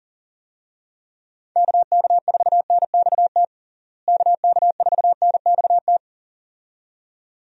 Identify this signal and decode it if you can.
Morse code, heard as a keyed tone, about 40 wpm, 700 Hz. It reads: KK4NXT KK4NXT
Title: USA Call Signs - 6 Characters - Repeated 2x - 40wpm